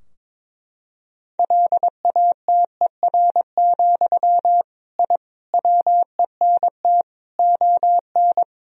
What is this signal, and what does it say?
Morse code, heard as a keyed tone, about 22 wpm, 700 Hz. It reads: LATER, I WENT ON
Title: Sentences from Top 1000 Words - Review Part 4 - 22wpm